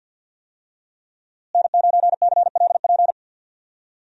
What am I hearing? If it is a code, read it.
Morse code, heard as a keyed tone, about 46 wpm, 700 Hz. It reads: N9CLP
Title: USA Call Signs - 5 Characters - Repeated 2x - 50wpm